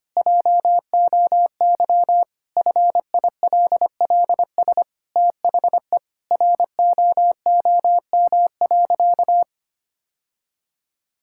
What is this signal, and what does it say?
Morse code, heard as a keyed tone, about 25 wpm, 700 Hz. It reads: JOY FILLS THE ROOM.
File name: practice / 25wpm / Sentences with 2-Word Phrases - 25wpm